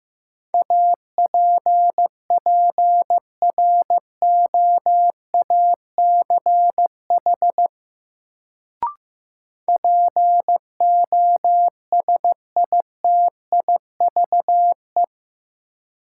Morse code, heard as a keyed tone, about 15 wpm, 700 Hz. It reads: APPROACH POSITIVE
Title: Sentences from Top 1400 Words - New Word Drill Alt - 15wpm